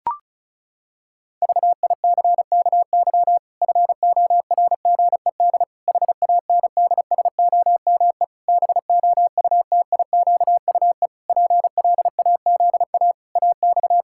Morse code, heard as a keyed tone, about 34 wpm, 700 Hz. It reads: VICKY FORGED HANDSOME BOUTIQUE PLAZA AX
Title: Pangram Challenge v2 - 35wpm